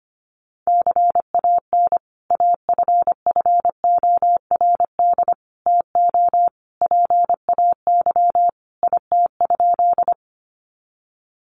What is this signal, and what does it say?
Morse code, heard as a keyed tone, about 25 wpm, 690 Hz. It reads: CAN AFFORD TO PAY IT?
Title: Sentences from Top 1100 Words 25wpm